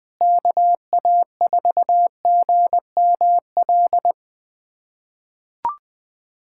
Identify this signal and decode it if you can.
Morse code, heard as a keyed tone, about 21 wpm, 700 Hz. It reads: KA4GML
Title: USA Callsign DE Callsign - 6 Characters - 20wpm